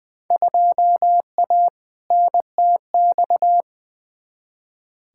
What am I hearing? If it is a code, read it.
Morse code, heard as a keyed tone, about 20 wpm, 700 Hz. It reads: 2A NTX
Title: Field Day Exchanges - 20wpm